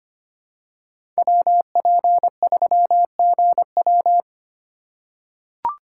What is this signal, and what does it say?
Morse code, heard as a keyed tone, about 25 wpm, 700 Hz. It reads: WP3GW
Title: USA Call Signs - 5 Characters - 3x Character Spacing - 25wpm